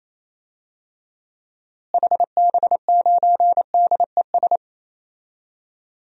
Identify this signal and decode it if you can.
Morse code, heard as a keyed tone, about 28 wpm, 700 Hz. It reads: HB9DES
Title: DXCC-100 Call Signs - 6 Characters - Repeated 2x - 28wpm